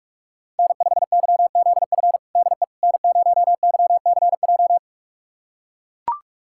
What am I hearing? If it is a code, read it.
Morse code, heard as a keyed tone, about 42 wpm, 700 Hz. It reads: N5YZF DE N0YCJ
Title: USA Callsign DE Callsign - 5 Characters - 45wpm